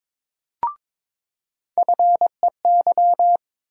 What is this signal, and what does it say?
Morse code, heard as a keyed tone, about 22 wpm, 700 Hz. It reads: FEY
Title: USA Call Sign Postfixes - 3 Letters 22wpm